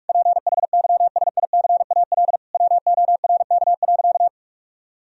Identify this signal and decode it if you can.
Morse code, heard as a keyed tone, about 46 wpm, 700 Hz. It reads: PHYSICAL WORK.
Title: Sentences from Top 1000 Words - Review Part 4 - 45wpm